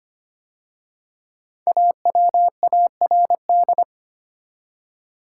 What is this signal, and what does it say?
Morse code, heard as a keyed tone, about 25 wpm, 700 Hz. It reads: AWARD
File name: 5 Letter Words - First 3 Letters Given - 25wpm